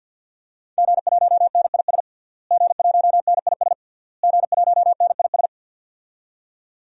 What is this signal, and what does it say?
Morse code, heard as a keyed tone, about 53 wpm, 700 Hz. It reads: G1NIS G1NIS G1NIS
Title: DXCC-100 Call Signs - 5 Characters - Repeated 3x - 50wpm